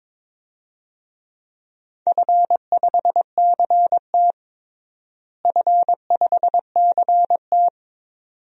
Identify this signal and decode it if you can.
Morse code, heard as a keyed tone, about 22 wpm, 700 Hz. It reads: F5CT F5CT
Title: DXCC-100 Call Signs - 4 Characters - Repeated 2x - 22wpm